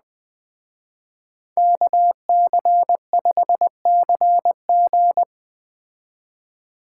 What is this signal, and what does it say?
Morse code, heard as a keyed tone, about 20 wpm, 700 Hz. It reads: KC5CG